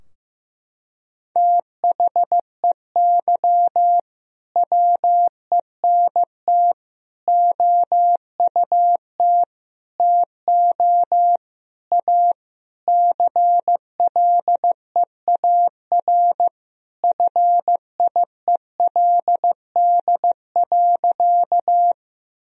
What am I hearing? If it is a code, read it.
Morse code, heard as a keyed tone, about 16 wpm, 700 Hz. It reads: THEY WENT OUT TO A CLEAR FIELD.
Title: Sentences from Top 1000 Words - Review Part 2 - 15wpm